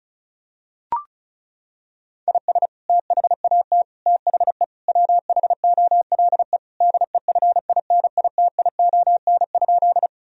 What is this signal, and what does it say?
Morse code, heard as a keyed tone, about 34 wpm, 700 Hz. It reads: IS THAT THE WHOLE DEFINITION?
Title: Sentences from Top 1000 Words - Review Part 4 - 35wpm